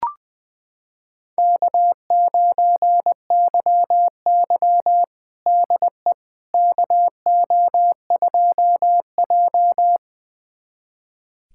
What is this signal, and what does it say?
Morse code, heard as a keyed tone, about 21 wpm, 700 Hz. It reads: K9YY DE KO2J
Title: USA Callsign DE Callsign - 4 Characters - 20wpm